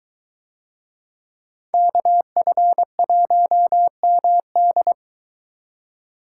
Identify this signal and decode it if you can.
Morse code, heard as a keyed tone, about 21 wpm, 700 Hz. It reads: KF1MD